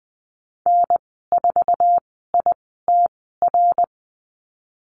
Morse code, heard as a keyed tone, about 20 wpm, spaced as for 15 wpm, 700 Hz. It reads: N4ITR
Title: 069 Lesson 12g - Letter d - Call Signs 20wpm